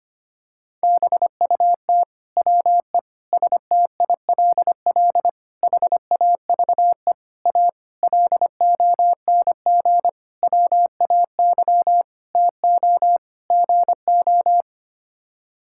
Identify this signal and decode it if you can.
Morse code, heard as a keyed tone, about 25 wpm, 700 Hz. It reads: BUT WE STILL HAVE A LONG WAY TO GO
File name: Sets of 9 Words - 25wpm